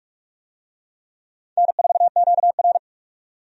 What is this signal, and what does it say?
Morse code, heard as a keyed tone, about 42 wpm, 700 Hz. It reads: N4QR
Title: USA Call Signs - 4 Characters - 45wpm